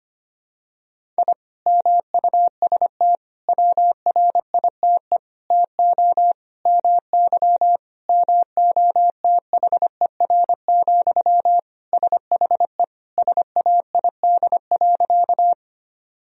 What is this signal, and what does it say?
Morse code, heard as a keyed tone, about 25 wpm, 700 Hz. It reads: I MUST WRITE TO MY MOTHER, SHE SAID.
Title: Sentences from Top 1000 Words - Review Part 1 - 25wpm